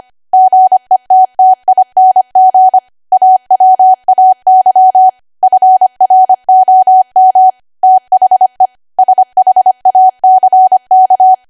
Morse code, heard as a keyed tone, about 25 wpm, 750 Hz. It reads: GETTING AWAY FROM THE SHACK
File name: Archive / 25 WPM / 25WPM